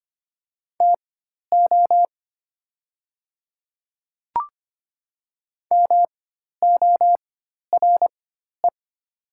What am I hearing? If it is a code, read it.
Morse code, heard as a keyed tone, about 25 wpm, spaced as for 12 wpm, 700 Hz. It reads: TO MORE
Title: 4x Character Spacing - Top 100 Words - 25wpm